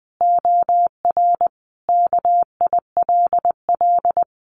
Morse code, heard as a keyed tone, about 20 wpm, 700 Hz. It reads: OR KILL
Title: Sentences from Top 1400 Words 20wpm